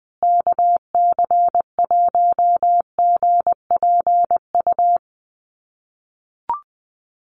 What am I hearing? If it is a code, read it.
Morse code, heard as a keyed tone, about 20 wpm, 700 Hz. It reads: KC1GPU
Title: USA Call Signs Repeated 2x 20wpm